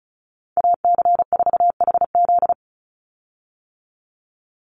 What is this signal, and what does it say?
Morse code, heard as a keyed tone, about 34 wpm, 700 Hz. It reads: AC4HZ
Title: USA Call Signs 35wpm